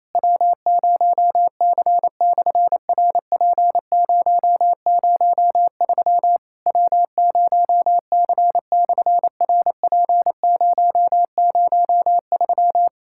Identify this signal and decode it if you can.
Morse code, heard as a keyed tone, about 28 wpm, 700 Hz. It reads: W0C/RP003 W0C/RP003
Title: SOTA QSO Practice 28wpm